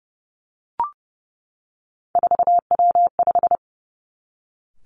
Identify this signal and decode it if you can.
Morse code, heard as a keyed tone, about 30 wpm, 700 Hz. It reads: 4W5
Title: Three Challenging Letter-Number Combinations 30wpm